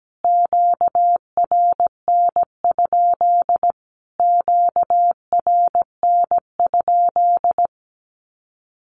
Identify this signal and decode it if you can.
Morse code, heard as a keyed tone, about 17 wpm, 700 Hz. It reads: QRN? QRN?
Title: Q-Codes Repeated 3x 17wpm